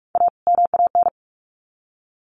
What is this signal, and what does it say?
Morse code, heard as a keyed tone, about 44 wpm, 700 Hz. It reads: A MAN
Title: Sets of 2 Words 45wpm